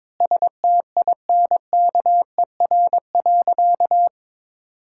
Morse code, heard as a keyed tone, about 21 wpm, 700 Hz. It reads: STINKER.
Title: Pangram Challenge v2 - 22wpm